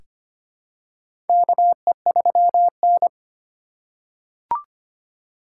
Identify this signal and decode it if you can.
Morse code, heard as a keyed tone, about 25 wpm, 700 Hz. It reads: KE3N
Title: USA Call Signs - 4 Characters - 4x Character Spacing - 25wpm